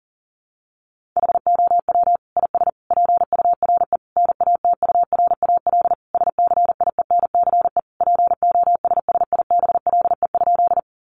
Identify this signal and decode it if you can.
Morse code, heard as a keyed tone, about 37 wpm, 700 Hz. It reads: HOW IS PURE NATURAL SCIENCE POSSIBLE?
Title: Sentences from Top 1500 Words 40wpm